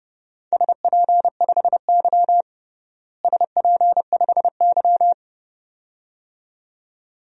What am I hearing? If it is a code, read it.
Morse code, heard as a keyed tone, about 30 wpm, 700 Hz. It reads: SP5Y SP5Y
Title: DXCC-100 Call Signs - 4 Characters - Repeated 2x - 30wpm